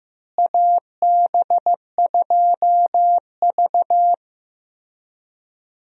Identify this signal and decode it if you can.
Morse code, heard as a keyed tone, about 15 wpm, 700 Hz. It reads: AB2V
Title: USA Call Signs - 4 Characters - Repeated 2x - 15wpm